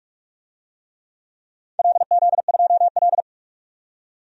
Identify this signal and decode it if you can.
Morse code, heard as a keyed tone, about 41 wpm, 700 Hz. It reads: LZ2L
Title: DXCC-100 Call Signs - 4 Characters - 45wpm